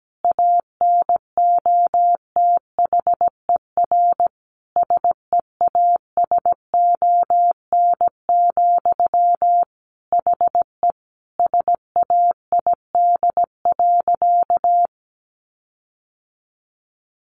Morse code, heard as a keyed tone, about 17 wpm, 700 Hz. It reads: ANOTHER SEASON, HE SAID.
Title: Sentences from Top 1400 Words 17wpm